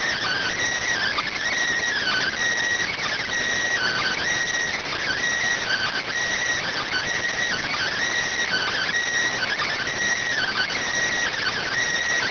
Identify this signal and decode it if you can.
Emergency Alert System (SAME) tones (header bursts), heard as harsh buzzing bursts. It is heard from 1.0 to 11.5 s.